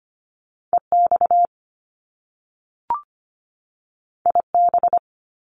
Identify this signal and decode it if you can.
Morse code, heard as a keyed tone, about 25 wpm, 700 Hz. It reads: EX IB
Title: Top Two Letter Combinations in Words 25wpm